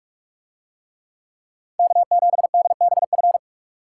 Morse code, heard as a keyed tone, about 46 wpm, 700 Hz. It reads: K7DBF